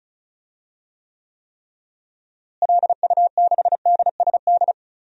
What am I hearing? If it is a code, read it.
Morse code, heard as a keyed tone, about 34 wpm, 700 Hz. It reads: LU6DSD
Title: DXCC-100 Call Signs - 6 Characters - Repeated 3x - 35wpm